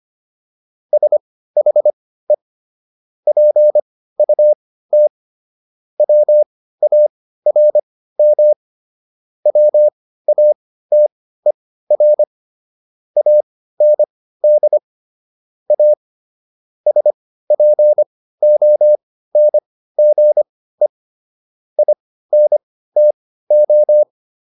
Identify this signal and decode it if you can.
Morse code, heard as a keyed tone, about 25 wpm, spaced as for 15 wpm, 600 Hz. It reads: SHE PUT WARM WATER AND A SPONGE INTO